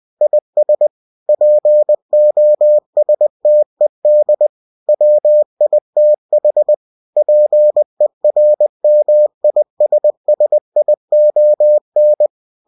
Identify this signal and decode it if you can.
Morse code, heard as a keyed tone, about 20 wpm, 600 Hz. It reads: IS POSTED WITH PERMISSION
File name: cw-books / winnie-the-pooh / winnie-20wpm-page-0115